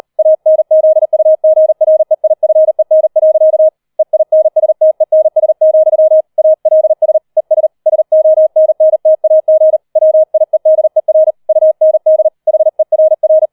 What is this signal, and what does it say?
Morse code, heard as a keyed tone, about 38 wpm, 630 Hz. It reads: ANZUGREIFEN. EINSTENS, ALS ES SONNTAG WIEDER UND HERR
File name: 004 Vierter Streich 40wpm